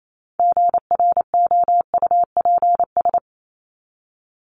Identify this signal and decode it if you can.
Morse code, heard as a keyed tone, about 28 wpm, 700 Hz. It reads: GROUPS
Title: ES:S Suffix - Top 100 Words with Base Words 28wpm